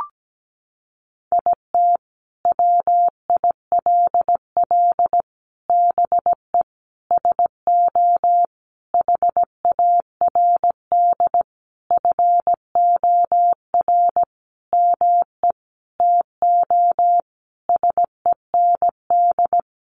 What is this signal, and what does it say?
Morse code, heard as a keyed tone, about 17 wpm, 700 Hz. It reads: IT WILL BE SO HARD FOR ME TO SEND
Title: Sentences from Top 1500 Words 17wpm